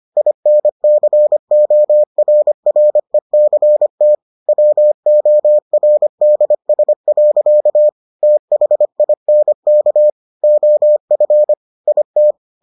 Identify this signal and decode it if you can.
Morse code, heard as a keyed tone, about 25 wpm, 600 Hz. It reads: INCORRECT WORDS. THINK OF IT